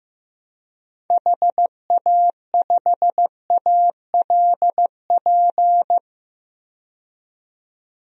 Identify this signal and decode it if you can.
Morse code, heard as a keyed tone, about 16 wpm, 700 Hz. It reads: HA5ALP
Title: DXCC-100 Call Signs - 6 Characters - 15wpm